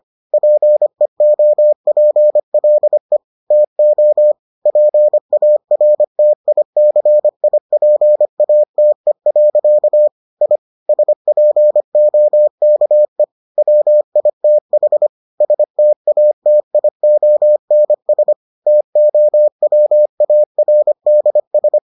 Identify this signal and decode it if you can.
Morse code, heard as a keyed tone, about 25 wpm, 600 Hz. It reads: PEOPLE TO PARTICIPATE. I SPOKE WITH STATIONS TOWARDS